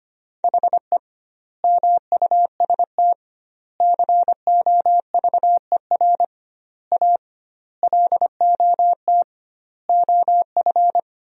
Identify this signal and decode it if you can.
Morse code, heard as a keyed tone, about 25 wpm, 700 Hz. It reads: HE MUST COVER A LOT OF
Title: 2x Word-Spacing - Sentences from Top 500 Words - 25wpm